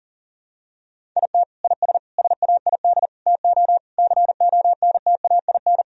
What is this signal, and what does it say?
Morse code, heard as a keyed tone, about 40 wpm, 700 Hz. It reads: IT IS SAID TO CONTAIN
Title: Sentences from Top 1000 Words - Review Part 2 - 40wpm